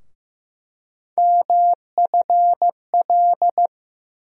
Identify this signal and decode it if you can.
Morse code, heard as a keyed tone, about 15 wpm, 700 Hz. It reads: MFL
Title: USA Call Sign Postfixes - 3 Letters 15wpm